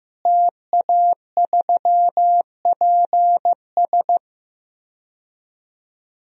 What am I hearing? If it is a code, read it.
Morse code, heard as a keyed tone, about 14 wpm, 700 Hz. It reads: TA3PS
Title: DXCC-100 Call Signs - 5 Characters - Repeated 3x - 15wpm